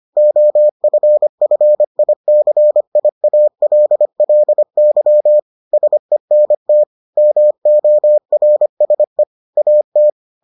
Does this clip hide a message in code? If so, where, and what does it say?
Yes: Morse code from 0.2 to 10.1 s, heard as a keyed tone, about 25 wpm, 600 Hz. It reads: OFFICIALLY SENT MORSE AT